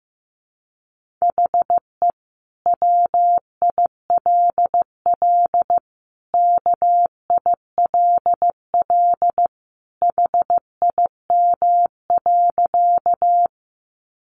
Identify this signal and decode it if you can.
Morse code, heard as a keyed tone, about 15 wpm, 700 Hz. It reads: HE WILL KILL HIM.
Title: Sentences from Top 1400 Words 15wpm